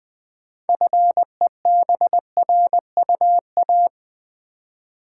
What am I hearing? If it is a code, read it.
Morse code, heard as a keyed tone, about 20 wpm, 700 Hz. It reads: FEBRUA
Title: Build A Word - 2 Letters At A Time - 8-Letter Words - 20wpm